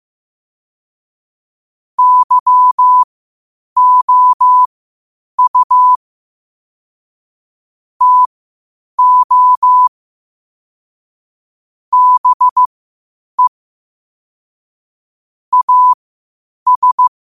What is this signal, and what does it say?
Morse code, heard as a keyed tone, about 15 wpm, spaced as for 8 wpm, 1000 Hz. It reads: YOU TO BE AS
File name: qotdfiles / 8wpm